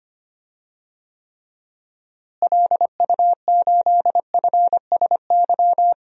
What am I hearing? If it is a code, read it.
Morse code, heard as a keyed tone, about 25 wpm, 700 Hz. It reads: LU8FSY
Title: DXCC-100 Call Signs - 6 Characters - Repeated 2x - 25wpm